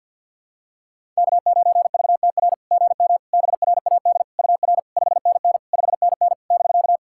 Morse code, heard as a keyed tone, about 53 wpm, 690 Hz. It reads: K9VTR GM BRAD UR 5NN 5NN <BK>
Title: SOTA QSO Practice 50wpm